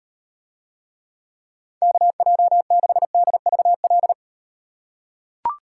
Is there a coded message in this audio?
Yes: Morse code from 1.8 to 4.1 s, heard as a keyed tone, about 38 wpm, 700 Hz. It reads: KJ6DVL